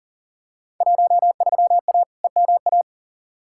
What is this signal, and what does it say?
Morse code, heard as a keyed tone, about 37 wpm, 700 Hz. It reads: 13A EMA
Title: Field Day Exchanges Encore - 40wpm